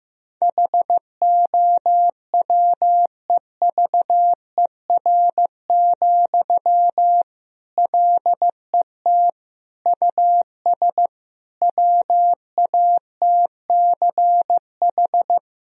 Morse code, heard as a keyed tone, about 15 wpm, 700 Hz. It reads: HOWEVER, LET US WATCH
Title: Sentences from Top 1000 Words - Review Part 1 - 15wpm